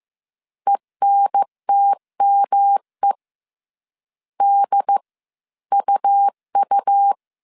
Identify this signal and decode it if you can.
Morse code, heard as a keyed tone, about 15 wpm, spaced as for 13 wpm, 800 Hz. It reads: ENTME DUU
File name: CW / DET / 1 3 1500C S C 14wpm 2458